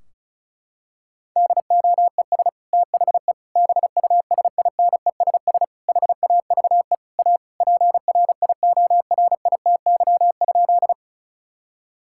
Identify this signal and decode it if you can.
Morse code, heard as a keyed tone, about 34 wpm, 700 Hz. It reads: DOES THE BUSINESS HAVE A PRIORITY?
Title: Sentences from Top 1600 Words 35wpm